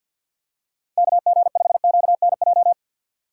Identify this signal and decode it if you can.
Morse code, heard as a keyed tone, about 47 wpm, 700 Hz. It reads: KG5QNJ
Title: USA Call Signs - 6 Characters - 3x Character Spacing - 50wpm